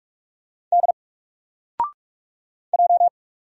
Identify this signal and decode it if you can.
Morse code, heard as a keyed tone, about 44 wpm, 700 Hz. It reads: D J
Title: Single Character (Letter:Number:Punctuation:Prosign) Alt 45wpm